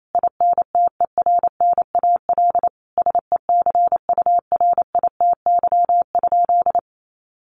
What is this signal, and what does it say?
Morse code, heard as a keyed tone, about 28 wpm, 700 Hz. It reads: INTERNAL SECURITY?
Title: Sentences from Top 1400 Words 28wpm